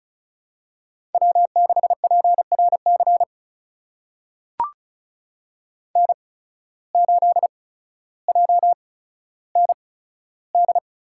Morse code, heard as a keyed tone, about 36 wpm, spaced as for 15 wpm, 700 Hz. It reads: W6PRC N8JND
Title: USA Call Signs - 5 Characters - 8x Character Spacing - 35wpm